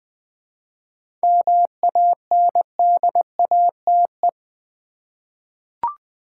Morse code, heard as a keyed tone, about 20 wpm, 700 Hz. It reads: MANDATE
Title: 7 Letter Words - First 3 Letters Given - 20wpm